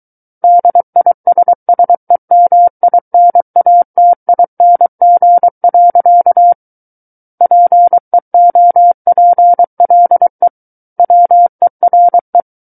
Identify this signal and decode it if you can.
Morse code, heard as a keyed tone, about 23 wpm, 700 Hz. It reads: DISSEMINATING. PEOPLE WERE
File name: Book One / 14-Chapter 14 - 23 WPM